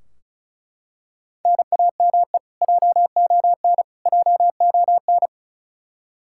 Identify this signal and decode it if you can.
Morse code, heard as a keyed tone, about 34 wpm, 700 Hz. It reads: NAME JON JON
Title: K1USN SST QSO Practice 35wpm